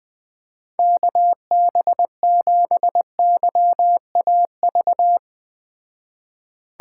Morse code, heard as a keyed tone, about 20 wpm, 700 Hz. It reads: KB7YAV
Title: Speed Racing - USA 2x3 Call Signs 17wpm